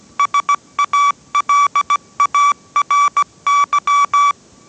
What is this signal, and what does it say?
Morse code, heard as a keyed tone, about 17 wpm, 1150 Hz. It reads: SALARY